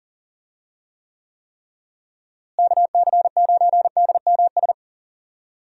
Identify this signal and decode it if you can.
Morse code, heard as a keyed tone, about 40 wpm, 700 Hz. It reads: KC9DMS
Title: USA Call Signs - 6 Characters - Repeated 3x - 40wpm